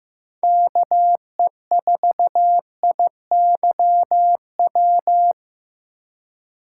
Morse code, heard as a keyed tone, about 15 wpm, 700 Hz. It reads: KE4IYW
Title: USA Call Signs - 6 Characters - 3x Character Spacing - 15wpm